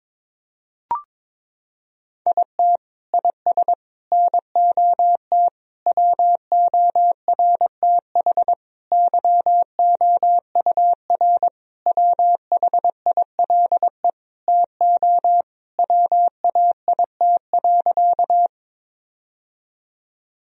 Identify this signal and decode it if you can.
Morse code, heard as a keyed tone, about 22 wpm, 700 Hz. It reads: IT IS NOT WORTH YOUR WHILE TO WAIT.
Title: Sentences from Top 1000 Words - Review Part 1 - 22wpm